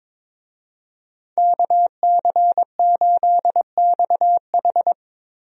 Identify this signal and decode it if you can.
Morse code, heard as a keyed tone, about 22 wpm, 700 Hz. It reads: KC8XH
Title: USA Call Signs - 5 Characters - Repeated 3x - 22wpm